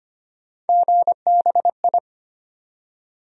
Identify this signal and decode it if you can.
Morse code, heard as a keyed tone, about 25 wpm, 700 Hz. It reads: GBI
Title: USA Call Sign Postfixes - 3 Letters 25wpm